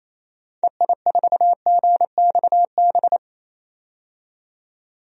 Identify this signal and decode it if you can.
Morse code, heard as a keyed tone, about 28 wpm, 700 Hz. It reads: EI4GXB
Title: DXCC-100 Call Signs - 6 Characters - 28wpm